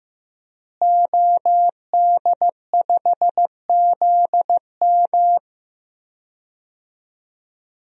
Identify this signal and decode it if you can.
Morse code, heard as a keyed tone, about 15 wpm, 700 Hz. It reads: OD5ZM
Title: DXCC-100 Call Signs - 5 Characters - Repeated 3x - 15wpm